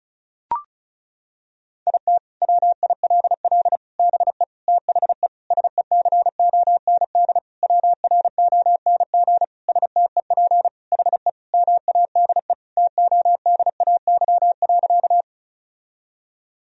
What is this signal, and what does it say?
Morse code, heard as a keyed tone, about 36 wpm, 700 Hz. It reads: IT WILL BE THE SECOND WRONG STEP HE MADE TODAY.
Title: Sentences from Top 1000 Words - Review Part 2 - 35wpm